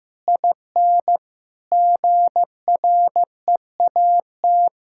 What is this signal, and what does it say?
Morse code, heard as a keyed tone, about 15 wpm, 700 Hz. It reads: IN GREAT
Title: Sentences from Top 1600 Words 15wpm